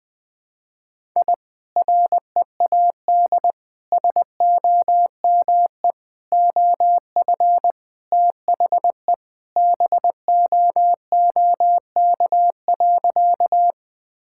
Morse code, heard as a keyed tone, about 21 wpm, 700 Hz. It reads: I READ SOME OF THE BOOK.
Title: Sentences with 2-Word Phrases - Spoken Beginning - 20wpm